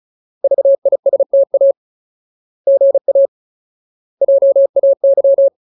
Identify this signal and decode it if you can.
Morse code, heard as a keyed tone, about 37 wpm, 550 Hz. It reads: VISTA GA JAY
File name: Ultimate Word List / 35WPM Ultimate